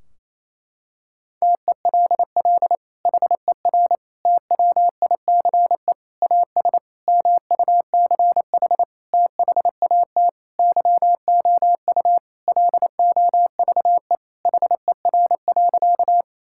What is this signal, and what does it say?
Morse code, heard as a keyed tone, about 28 wpm, 700 Hz. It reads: TELL HER TWICE AS MUCH THAT YOU LOVE HER.
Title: Sentences from Top 1600 Words 28wpm